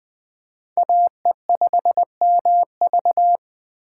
Morse code, heard as a keyed tone, about 19 wpm, 700 Hz. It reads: AE5MV